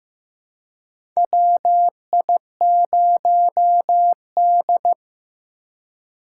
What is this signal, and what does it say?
Morse code, heard as a keyed tone, about 15 wpm, 700 Hz. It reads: WI0D
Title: USA Call Signs - 4 Characters - 4x Character Spacing - 15wpm